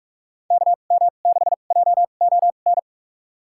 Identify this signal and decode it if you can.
Morse code, heard as a keyed tone, about 44 wpm, spaced as for 32 wpm, 700 Hz. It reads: KM6JON
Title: USA Call Signs - 6 Characters - 2x Character Spacing - 45wpm